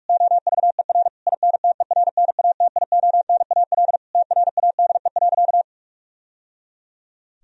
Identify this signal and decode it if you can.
Morse code, heard as a keyed tone, about 46 wpm, 700 Hz. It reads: OVER INTERNATIONAL TRADE.
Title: Sentences from Top 1000 Words - Review Part 4 - 45wpm